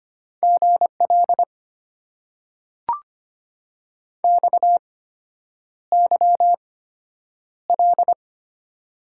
Morse code, heard as a keyed tone, about 25 wpm, spaced as for 9 wpm, 690 Hz. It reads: GL XYL
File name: Top 90 QSO Elements - 8x Character Spacing - 25wpm